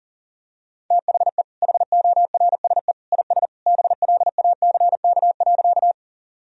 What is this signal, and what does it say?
Morse code, heard as a keyed tone, about 40 wpm, 700 Hz. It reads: THE HORSE IS BLACK.
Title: Sentences from Top 1000 Words - Review Part 2 - 40wpm